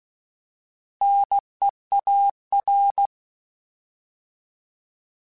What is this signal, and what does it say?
Morse code, heard as a keyed tone, about 16 wpm, 800 Hz. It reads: NEAR